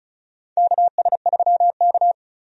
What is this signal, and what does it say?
Morse code, heard as a keyed tone, about 36 wpm, 700 Hz. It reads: KS3K
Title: USA Call Signs - 4 Characters - 35wpm